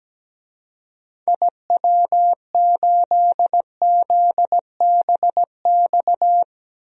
Morse code, heard as a keyed tone, about 17 wpm, 700 Hz. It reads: IW8ZBX
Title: DXCC-100 Call Signs - 6 Characters - 17wpm